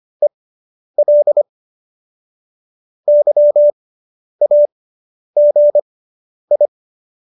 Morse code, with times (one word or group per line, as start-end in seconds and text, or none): 0.2-1.4 EL
3.1-6.7 YAGI